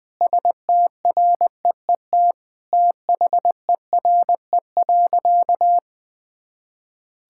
Morse code, heard as a keyed tone, about 20 wpm, 700 Hz. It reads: STREET THERE.
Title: Sentences from Top 1000 Words - Review Part 5 - 20wpm